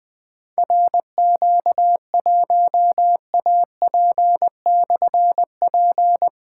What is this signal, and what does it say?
Morse code, heard as a keyed tone, about 20 wpm, 700 Hz. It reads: RQ1AP/P
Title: Difficult Call Signs 20wpm